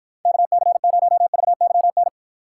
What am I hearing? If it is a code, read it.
Morse code, heard as a keyed tone, about 45 wpm, 700 Hz. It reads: KC0VYN